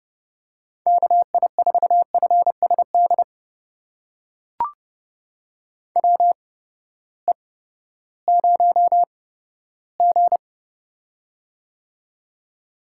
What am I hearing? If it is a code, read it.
Morse code, heard as a keyed tone, about 30 wpm, spaced as for 15 wpm, 700 Hz. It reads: KI4FSD WE0G
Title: USA Call Signs - 8x Character Spacing - 30wpm